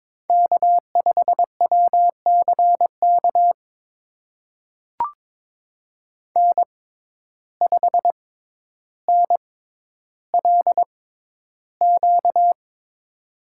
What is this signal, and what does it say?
Morse code, heard as a keyed tone, about 23 wpm, spaced as for 11 wpm, 700 Hz. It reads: K5WCK N5NLQ
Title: USA Call Signs - 5 Characters - 6x Character Spacing - 22wpm